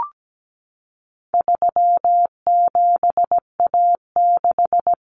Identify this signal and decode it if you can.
Morse code, heard as a keyed tone, about 17 wpm, 700 Hz. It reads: 37A6
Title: Common Cut Numbers - 4-Digit 17wpm